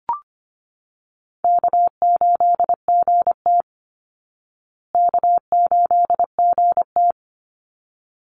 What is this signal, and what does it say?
Morse code, heard as a keyed tone, about 25 wpm, 700 Hz. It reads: K8GT K8GT
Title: Easy Call Signs Repeated 2x - 25wpm